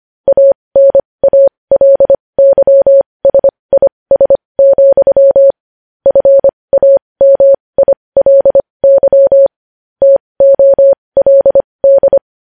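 Morse code, heard as a keyed tone, about 25 wpm, spaced as for 20 wpm, 550 Hz. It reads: ANALYSIS, FAMILY TOLD
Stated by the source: Morse code news headlines practice at 20 wpm.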